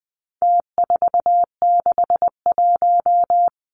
Morse code, heard as a keyed tone, about 20 wpm, 700 Hz. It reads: T461
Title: Numbers - Contest Style - 4-Digit 20wpm